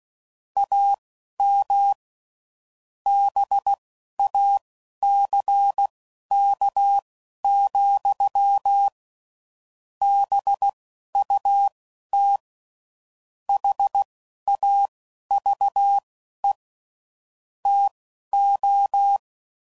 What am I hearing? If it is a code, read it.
Morse code, heard as a keyed tone, about 16 wpm, spaced as for 11 wpm, 800 Hz. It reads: AM BACK, BUT HAVE TO
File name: CW Code Groups @10WPM / QSOs / QSO kc7sac